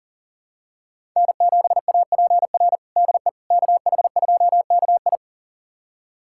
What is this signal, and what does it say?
Morse code, heard as a keyed tone, about 40 wpm, 700 Hz. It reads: N7APR DE KH2KI
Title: USA Callsign DE Callsign - 5 Characters - 40wpm